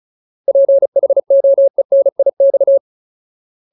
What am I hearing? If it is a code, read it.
Morse code, heard as a keyed tone, about 36 wpm, 550 Hz. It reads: PHOENIX
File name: Ultimate Word List / 35WPM Ultimate